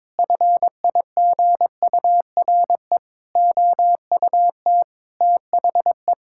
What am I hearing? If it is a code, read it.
Morse code, heard as a keyed tone, about 22 wpm, 700 Hz. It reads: FIGURE OUT THE
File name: practice / 22wpm / Sentences with 4-Word Phrases - 22wpm